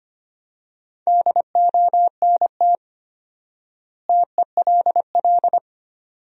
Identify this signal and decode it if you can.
Morse code, heard as a keyed tone, about 25 wpm, 700 Hz. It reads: DONT TELL
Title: 4x Word-Spacing - Sentences from Top 300 Words - 25wpm